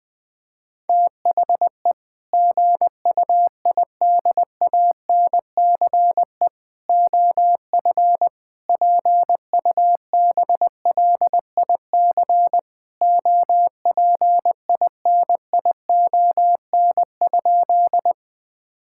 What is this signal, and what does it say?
Morse code, heard as a keyed tone, about 20 wpm, 700 Hz. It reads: THE GUIDANCE OF PUBLIC OPINION?
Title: Sentences from Top 1600 Words 20wpm